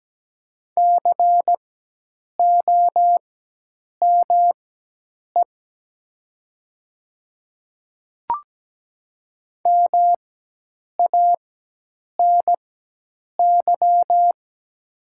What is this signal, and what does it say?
Morse code, heard as a keyed tone, about 17 wpm, spaced as for 8 wpm, 700 Hz. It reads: COME MANY
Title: 4x Character Spacing - Top 100 Words - 17wpm